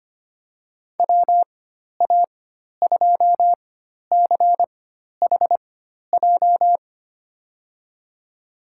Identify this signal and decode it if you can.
Morse code, heard as a keyed tone, about 25 wpm, spaced as for 12 wpm, 700 Hz. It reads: WA2CHJ
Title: USA Call Signs - 6 Characters - 4x Character Spacing - 25wpm